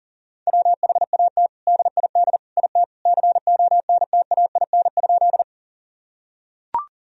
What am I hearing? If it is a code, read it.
Morse code, heard as a keyed tone, about 40 wpm, 700 Hz. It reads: WHAT DID IT CONTAIN?
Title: Sentences from Top 1000 Words - Review Part 2 - 40wpm